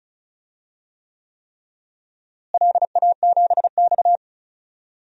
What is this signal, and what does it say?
Morse code, heard as a keyed tone, about 34 wpm, 700 Hz. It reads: LA7X
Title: DXCC-100 Call Signs - 4 Characters - Repeated 3x - 35wpm